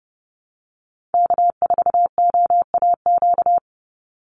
Morse code, heard as a keyed tone, about 30 wpm, 700 Hz. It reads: K4OAQ
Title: Call Signs Repeated 3x - 30wpm